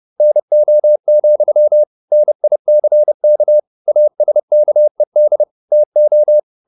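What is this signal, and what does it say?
Morse code, heard as a keyed tone, about 30 wpm, 600 Hz. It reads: NO, NICK ASKED TO
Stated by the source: Morse code at 30 wpm, the novel Never by Ken Follett in English, from CW med Gnister.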